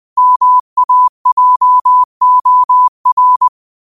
Morse code, heard as a keyed tone, about 20 wpm, 1000 Hz. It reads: MAJOR
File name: qotdfiles / 20wpm